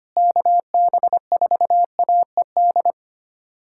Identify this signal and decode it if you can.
Morse code, heard as a keyed tone, about 25 wpm, 700 Hz. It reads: KB4AED
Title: USA Callsign DE Callsign - 6 Characters - 25wpm